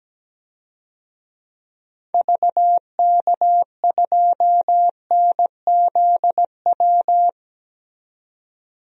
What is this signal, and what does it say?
Morse code, heard as a keyed tone, about 17 wpm, 700 Hz. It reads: VK2NZW
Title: DXCC-100 Call Signs - 6 Characters - 17wpm